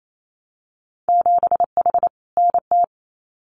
Morse code, heard as a keyed tone, about 28 wpm, 700 Hz. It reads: 7H NT
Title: Winter Field Day Exchanges Encore 28wpm